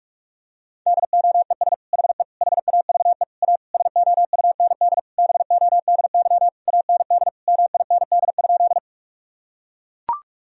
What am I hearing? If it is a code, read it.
Morse code, heard as a keyed tone, about 41 wpm, 700 Hz. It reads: DOES HE HAVE A SOUND BODY AND MIND?
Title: Sentences from Top 1000 Words - Review Part 2 - 45wpm